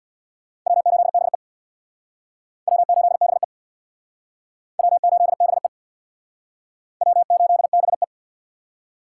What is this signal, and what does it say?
Morse code, heard as a keyed tone, about 64 wpm, 700 Hz. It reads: W8BE W8BE W8BE W8BE